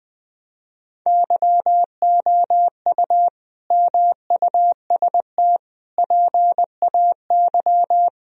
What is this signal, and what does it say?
Morse code, heard as a keyed tone, about 20 wpm, 700 Hz. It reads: YOU MUST PAY
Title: Sentences from Top 1000 Words - Review Part 2 - 20wpm